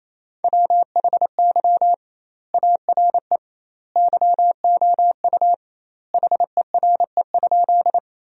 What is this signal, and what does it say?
Morse code, heard as a keyed tone, about 28 wpm, 700 Hz. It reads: WHY ARE YOU HERE?
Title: 2x Word-Spacing - Sentences from Top 200 Words - 28wpm